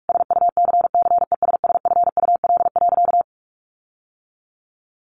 Morse code, heard as a keyed tone, about 44 wpm, 700 Hz. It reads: SUCCESSFUL.
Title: Sentences from Top 1300 Words 45wpm